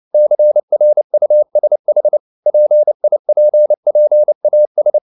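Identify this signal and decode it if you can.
Morse code, heard as a keyed tone, about 29 wpm, 600 Hz. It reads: CRUSH PIPPAS
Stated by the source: Morse code practice at 28 wpm, text of Ken Follett's novel Never in English, from CW med Gnister.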